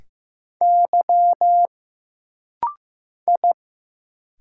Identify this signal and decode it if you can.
Morse code, heard as a keyed tone, about 15 wpm, 700 Hz. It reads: Y I
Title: Single Letters - Rapid-Fire 15wpm